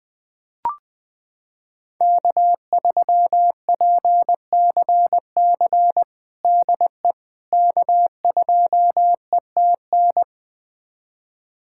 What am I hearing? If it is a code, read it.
Morse code, heard as a keyed tone, about 20 wpm, 700 Hz. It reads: K3PCC DE K2ETN
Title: USA Callsign DE Callsign - 5 Characters - 20wpm